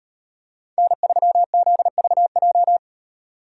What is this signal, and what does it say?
Morse code, heard as a keyed tone, about 32 wpm, 700 Hz. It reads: N3ZVJ